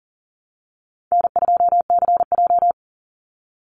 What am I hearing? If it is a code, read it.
Morse code, heard as a keyed tone, about 40 wpm, 700 Hz. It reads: N2CJ
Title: Easy Call Signs - 40wpm